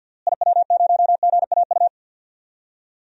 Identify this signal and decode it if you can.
Morse code, heard as a keyed tone, about 52 wpm, 700 Hz. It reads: IW0GAU
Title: DXCC-100 Call Signs - 6 Characters - Repeated 3x - 50wpm